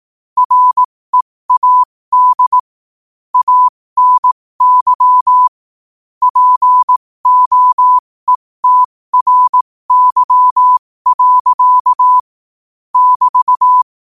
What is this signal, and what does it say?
Morse code, heard as a keyed tone, about 18 wpm, spaced as for 15 wpm, 1000 Hz. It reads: READ ANY POETRY. <BT>
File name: qotdfiles / 15wpm